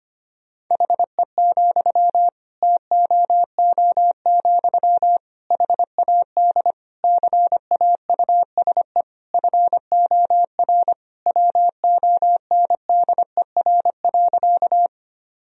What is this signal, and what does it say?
Morse code, heard as a keyed tone, about 25 wpm, 700 Hz. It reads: HE, TOO, HAD CAUSE FOR WONDER.
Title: Sentences from Top 1000 Words - Review Part 3 - 25wpm